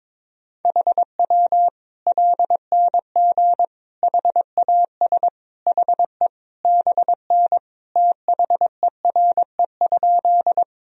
Morse code, heard as a keyed tone, about 22 wpm, 700 Hz. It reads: HW LNG HAS HE BN THERE?
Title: Sentences from Top 100 Words and Abbreviations 22wpm